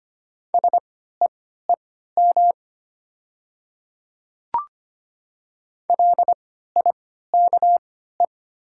Morse code, heard as a keyed tone, about 25 wpm, spaced as for 14 wpm, 700 Hz. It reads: SEEM LIKE
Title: 3x Character Spacing - Top 300 Words - 25wpm